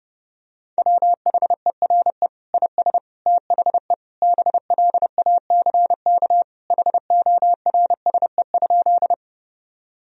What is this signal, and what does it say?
Morse code, heard as a keyed tone, about 30 wpm, 700 Hz. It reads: WHERE IS THE BLACK HORSE?
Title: Sentences from Top 1000 Words - Review Part 2 - 30wpm